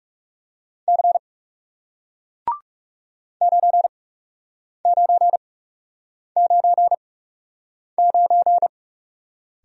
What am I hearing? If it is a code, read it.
Morse code, heard as a keyed tone, about 40 wpm, spaced as for 8 wpm, 700 Hz. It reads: C 9999